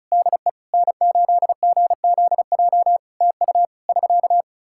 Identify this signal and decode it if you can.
Morse code, heard as a keyed tone, about 36 wpm, 700 Hz. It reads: DE N8GZJ TU <SK>
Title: K1USN SST QSO Practice 35wpm